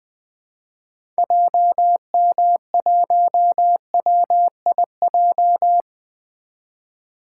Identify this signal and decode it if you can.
Morse code, heard as a keyed tone, about 20 wpm, 700 Hz. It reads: JM1WIJ
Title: 2023 POTA Activator Call Signs - 20wpm